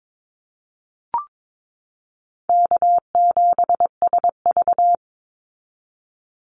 Morse code, heard as a keyed tone, about 22 wpm, 700 Hz. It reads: K7SV
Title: Easy Call Signs Repeated 2x - 22wpm